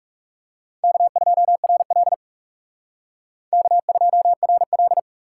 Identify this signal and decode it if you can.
Morse code, heard as a keyed tone, about 44 wpm, 700 Hz. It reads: K2RL K2RL